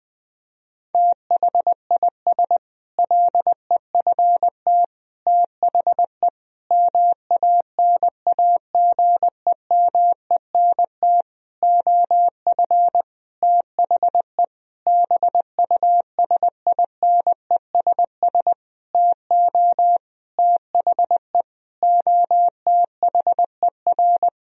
Morse code, heard as a keyed tone, about 20 wpm, 700 Hz. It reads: THIS LEFT THE MANAGEMENT OF THE BUSINESS TO THE OTHER
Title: Sentences from Top 1000 Words - Review Part 4 - 20wpm